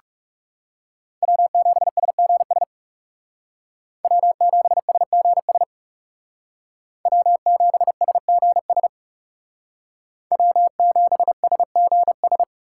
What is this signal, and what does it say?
Morse code, heard as a keyed tone, about 38 wpm, 700 Hz. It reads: W7SGS W7SGS W7SGS W7SGS